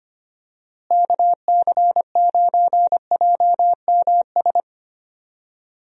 Morse code, heard as a keyed tone, about 25 wpm, 700 Hz. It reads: KC9JMS
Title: USA Call Signs - 6 Characters - Repeated 2x - 25wpm